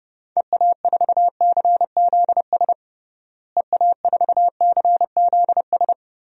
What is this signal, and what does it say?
Morse code, heard as a keyed tone, about 30 wpm, 700 Hz. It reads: EA4CZS EA4CZS
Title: DXCC-100 Call Signs - 6 Characters - Repeated 3x - 30wpm